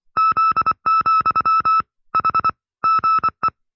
Morse code, heard as a keyed tone, about 24 wpm, 1310 Hz. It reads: Z, H GE
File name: FreeFiles / CW-practice / 20WPM-side2